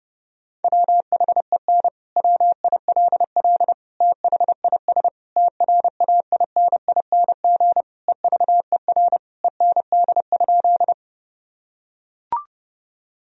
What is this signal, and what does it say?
Morse code, heard as a keyed tone, about 30 wpm, 700 Hz. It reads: WHEN WILL THIS TRAINING EVER END?
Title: Sentences from Top 1000 Words - Review Part 4 - 30wpm